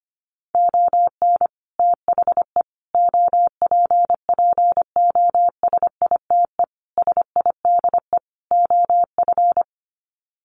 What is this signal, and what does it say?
Morse code, heard as a keyed tone, about 25 wpm, 700 Hz. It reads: ON THE OPPOSITE SIDE OF
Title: Sets of 5 Words Encore2 25wpm